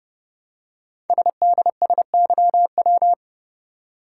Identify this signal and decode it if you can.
Morse code, heard as a keyed tone, about 30 wpm, 700 Hz. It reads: SDSYW
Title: Code Groups - 5 Letter:Numbers - 30wpm